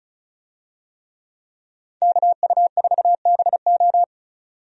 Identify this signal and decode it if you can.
Morse code, heard as a keyed tone, about 36 wpm, 700 Hz. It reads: KU4BO
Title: USA Call Signs - 5 Characters - 35wpm